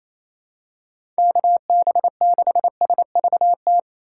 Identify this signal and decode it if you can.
Morse code, heard as a keyed tone, about 27 wpm, 700 Hz. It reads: KB6SVT
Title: USA Call Signs - 6 Characters - 6x Character Spacing - 28wpm